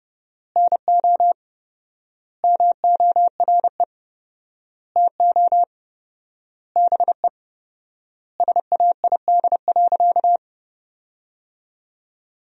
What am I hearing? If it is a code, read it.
Morse code, heard as a keyed tone, about 32 wpm, 700 Hz. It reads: NO MORE TO BE SAID.
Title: 4x Word-Spacing - Sentences from Top 100 Words - 30wpm